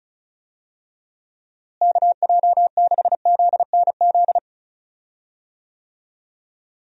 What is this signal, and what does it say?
Morse code, heard as a keyed tone, about 36 wpm, 700 Hz. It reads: KJ6ZNZ
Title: USA Call Signs - 6 Characters - Repeated 3x - 35wpm